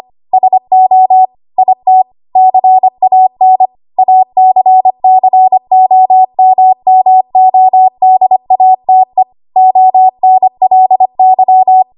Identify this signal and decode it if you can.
Morse code, heard as a keyed tone, about 25 wpm, 750 Hz. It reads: SO IT CAN ACCOMMODATE ONLY
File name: Archive / 25 WPM / 25WPM